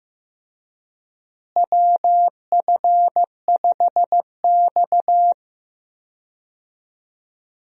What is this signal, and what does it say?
Morse code, heard as a keyed tone, about 15 wpm, 700 Hz. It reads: WF5X
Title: USA Call Signs - 4 Characters - Repeated 3x - 15wpm